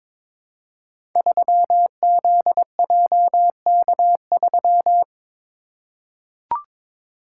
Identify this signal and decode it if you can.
Morse code, heard as a keyed tone, about 22 wpm, 700 Hz. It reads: 3ZJK3
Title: Code Groups - 5 Letter:Numbers - 22wpm